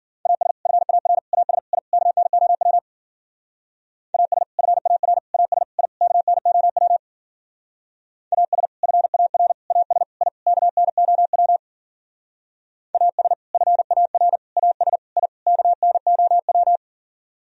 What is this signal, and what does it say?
Morse code, heard as a keyed tone, about 50 wpm, 700 Hz. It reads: AS FAR AS I KNOW AS FAR AS I KNOW AS FAR AS I KNOW AS FAR AS I KNOW
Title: Speed-Racing - Sets of 5 Words 40wpm